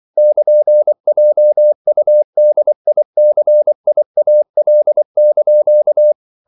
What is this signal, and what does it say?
Morse code, heard as a keyed tone, about 24 wpm, 600 Hz. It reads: <KN>JUDICIAL)
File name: CW / Practice / NationalCapitals24